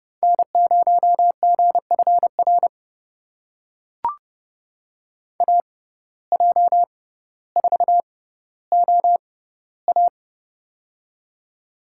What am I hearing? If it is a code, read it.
Morse code, heard as a keyed tone, about 32 wpm, spaced as for 16 wpm, 700 Hz. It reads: N0GFR AJ4OA
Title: USA Call Signs - 5 Characters - 6x Character Spacing - 30wpm